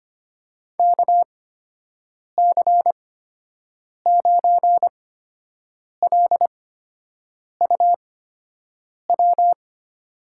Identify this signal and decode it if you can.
Morse code, heard as a keyed tone, about 25 wpm, spaced as for 7 wpm, 700 Hz. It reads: KC9LUW
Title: USA Call Signs - 6 Characters - 8x Character Spacing - 25wpm